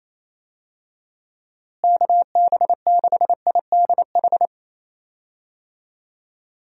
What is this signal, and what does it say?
Morse code, heard as a keyed tone, about 28 wpm, 700 Hz. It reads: KB6IDH
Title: USA Call Signs - 6 Characters - Repeated 2x - 28wpm